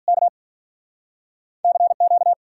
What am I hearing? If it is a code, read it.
Morse code, heard as a keyed tone, about 53 wpm, 700 Hz. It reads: K CQ